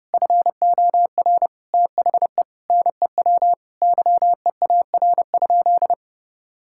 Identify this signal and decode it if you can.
Morse code, heard as a keyed tone, about 30 wpm, 700 Hz. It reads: FOR THE NEW YEAR?
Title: Sentences from Top 1000 Words - Review Part 5 - 30wpm